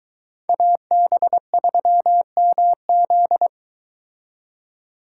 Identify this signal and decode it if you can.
Morse code, heard as a keyed tone, about 22 wpm, 700 Hz. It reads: AB3MZ